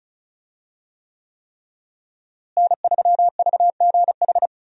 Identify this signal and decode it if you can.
Morse code, heard as a keyed tone, about 34 wpm, 700 Hz. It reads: N3VGH
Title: USA Call Signs - 5 Characters - Repeated 3x - 35wpm